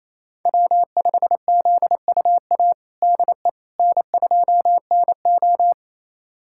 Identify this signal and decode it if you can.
Morse code, heard as a keyed tone, about 28 wpm, 700 Hz. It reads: W5ZUA DE N2NO
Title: USA Callsign DE Callsign 28wpm